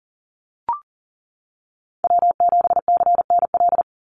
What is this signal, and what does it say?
Morse code, heard as a keyed tone, about 40 wpm, 700 Hz. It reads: W7CNL
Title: NA SOTA Chaser Callsigns 40wpm